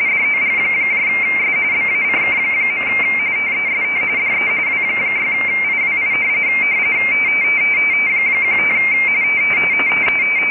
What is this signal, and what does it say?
RTTY, about 45.45 baud. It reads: DOGS BACK. TEST DE W6IWI 15
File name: rtty / websdr recording 14086.0kHz